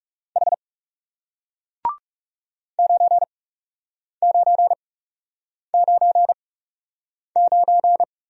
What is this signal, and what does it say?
Morse code, heard as a keyed tone, about 40 wpm, spaced as for 8 wpm, 700 Hz. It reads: H 9999